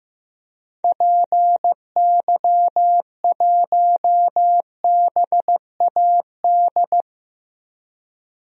Morse code, heard as a keyed tone, about 15 wpm, 700 Hz. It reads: PY1BAD
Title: DXCC-100 Call Signs - 6 Characters - Repeated 3x - 15wpm